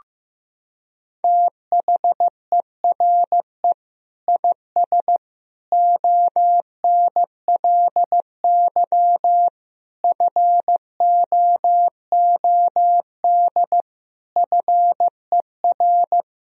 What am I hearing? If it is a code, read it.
Morse code, heard as a keyed tone, about 15 wpm, 700 Hz. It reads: THERE IS ONLY FOOD FER
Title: Sentences from Top 300 Words and Abbreviations 15wpm